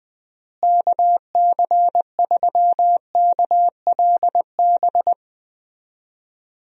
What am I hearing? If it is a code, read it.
Morse code, heard as a keyed tone, about 20 wpm, 700 Hz. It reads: KC3KLB
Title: USA Call Signs - 6 Characters - Repeated 3x - 20wpm